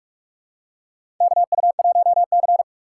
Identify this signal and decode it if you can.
Morse code, heard as a keyed tone, about 46 wpm, 700 Hz. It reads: KU1C
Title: USA Call Signs - 4 Characters - 6x Character Spacing - 45wpm